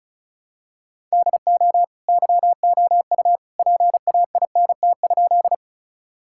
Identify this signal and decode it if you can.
Morse code, heard as a keyed tone, about 36 wpm, 700 Hz. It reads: DO YOU PAINT?
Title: Sentences from Top 1000 Words - Review Part 6 - 35wpm